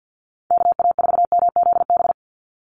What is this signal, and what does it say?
Morse code, heard as a keyed tone, about 50 wpm, 700 Hz. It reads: KA4MZB
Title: USA Call Signs 50wpm